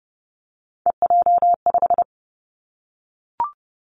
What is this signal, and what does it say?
Morse code, heard as a keyed tone, about 30 wpm, 700 Hz. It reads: EJ5
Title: Three Challenging Letter-Number Combinations 30wpm